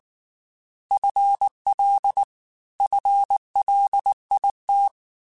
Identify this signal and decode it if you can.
Morse code, heard as a keyed tone, about 19 wpm, 800 Hz. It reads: FL FLIT
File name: FL / FL mp3s / FL Prefixes 1x1 @18